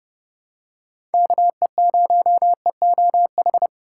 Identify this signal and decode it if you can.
Morse code, heard as a keyed tone, about 30 wpm, 700 Hz. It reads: KE0EOH
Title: USA Call Signs - 6 Characters - 2x Character Spacing - 30wpm